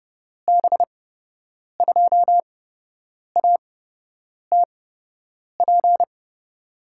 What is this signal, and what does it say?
Morse code, heard as a keyed tone, about 30 wpm, spaced as for 8 wpm, 700 Hz. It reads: B2ATP
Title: USA Call Signs - 6 Characters - 8x Character Spacing - 30wpm